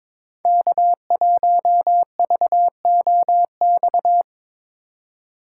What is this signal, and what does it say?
Morse code, heard as a keyed tone, about 25 wpm, 700 Hz. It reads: K1VOX
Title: Speed Racing - USA 1x3 Call Signs 22wpm